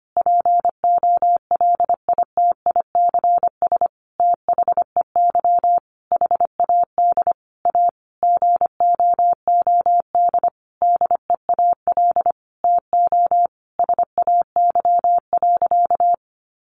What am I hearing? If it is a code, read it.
Morse code, heard as a keyed tone, about 25 wpm, 700 Hz. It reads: POLITICS THEY HAD A GOOD DEAL TO SAY.
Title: Sentences from Top 1000 Words 25wpm